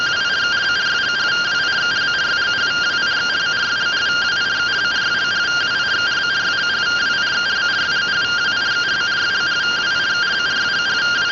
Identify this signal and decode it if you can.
Two-tone teleprinter signal, about 170 Hz shift.